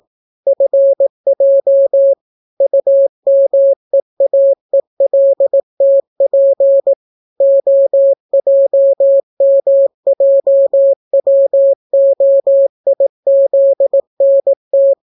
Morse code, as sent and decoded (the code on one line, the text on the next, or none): ..-. .--- / ..- -- . .- . .-.. - .--. / --- .--- -- .--- .-- --- .. --.. -. -
FJ UMEAELTP OJMJWOIZNT